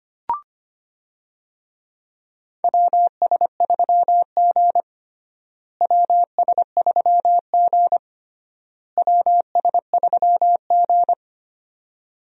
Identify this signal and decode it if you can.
Morse code, heard as a keyed tone, about 25 wpm, 700 Hz. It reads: WS3G WS3G WS3G
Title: USA Call Signs - 4 Characters - Repeated 3x - 25wpm